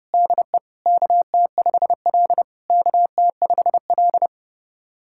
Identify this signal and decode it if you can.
Morse code, heard as a keyed tone, about 30 wpm, 690 Hz. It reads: DE KT5L KT5L
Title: SOTA QSO Practice 30wpm